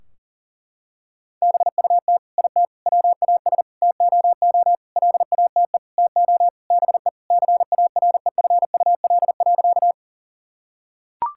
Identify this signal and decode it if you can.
Morse code, heard as a keyed tone, about 40 wpm, 700 Hz. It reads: BUT IT WAS TOO LATE TO BE CAREFUL.
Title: Sentences from Top 1600 Words 40wpm